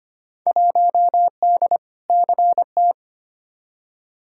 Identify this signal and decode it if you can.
Morse code, heard as a keyed tone, about 25 wpm, 700 Hz. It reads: 1D CT
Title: Field Day Exchanges - 25wpm